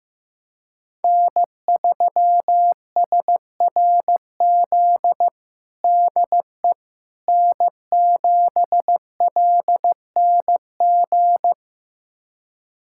Morse code, heard as a keyed tone, about 15 wpm, 700 Hz. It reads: N3SRZ DE N7LNG
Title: USA Callsign DE Callsign - 5 Characters - 15wpm